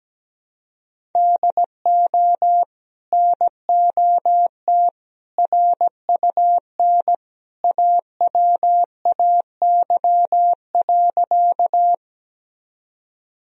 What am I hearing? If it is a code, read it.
Morse code, heard as a keyed tone, about 17 wpm, 700 Hz. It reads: DO NOT RUN AWAY.
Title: Sentences from Top 1000 Words - Review Part 1 - 17wpm